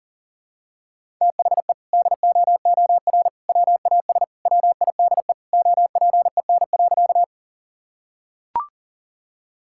Morse code, heard as a keyed tone, about 40 wpm, 700 Hz. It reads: THE DOOR WAS WIDE OPEN.
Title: Sentences from Top 1000 Words - Review Part 6 - 40wpm